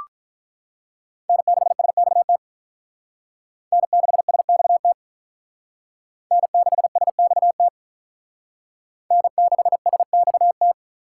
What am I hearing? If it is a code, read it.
Morse code, heard as a keyed tone, about 44 wpm, 700 Hz. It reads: N6SXT N6SXT N6SXT N6SXT